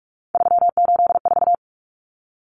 Morse code, heard as a keyed tone, about 44 wpm, 700 Hz. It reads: 384
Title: Common Cut Numbers - 3-Digit 45wpm